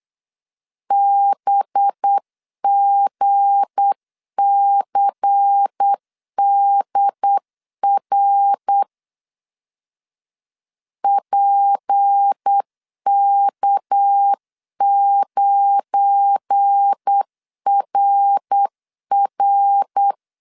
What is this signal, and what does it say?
Morse code, heard as a keyed tone, about 8 wpm, 800 Hz. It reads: BGCDR PK9RR
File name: CW / DET / 1 9 1500C S A 8wpm 5637